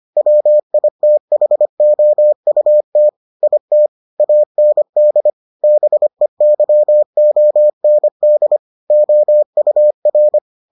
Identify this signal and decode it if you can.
Morse code, heard as a keyed tone, about 25 wpm, 600 Hz. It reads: WITHOUT IT AND BEYOND OUR